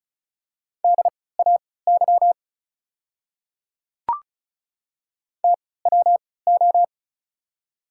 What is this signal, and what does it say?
Morse code, heard as a keyed tone, about 36 wpm, spaced as for 20 wpm, 690 Hz. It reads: DAY TWO
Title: 3x Character Spacing - Top 200 Words - 35wpm